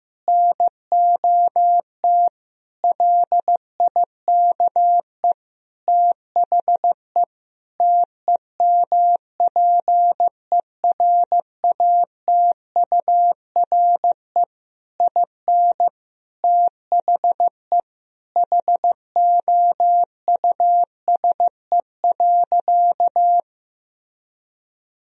Morse code, heard as a keyed tone, about 15 wpm, 700 Hz. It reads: NOT LIKE THE TEMPERATURE IN THE HOUSE.
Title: Sentences from Top 1000 Words - Review Part 4 - 15wpm